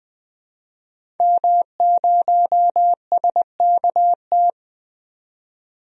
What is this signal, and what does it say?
Morse code, heard as a keyed tone, about 20 wpm, 700 Hz. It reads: M0SKT
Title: DXCC-100 Call Signs - 5 Characters - 20wpm